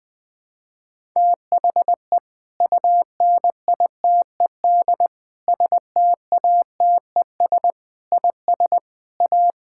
Morse code, heard as a keyed tone, about 20 wpm, 700 Hz. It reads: THE UNITED STATES IS A
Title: Sentences with 4-Word Phrases - Spoken Beginning - 20wpm